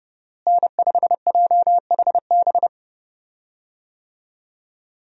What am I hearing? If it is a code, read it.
Morse code, heard as a keyed tone, about 30 wpm, 700 Hz. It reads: N5JHB
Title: USA Call Signs - 5 Characters - Repeated 3x - 30wpm